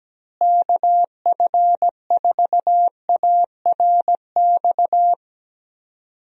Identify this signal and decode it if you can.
Morse code, heard as a keyed tone, about 17 wpm, 700 Hz. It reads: KF4ARX
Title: USA Call Signs - 6 Characters - Repeated 2x - 17wpm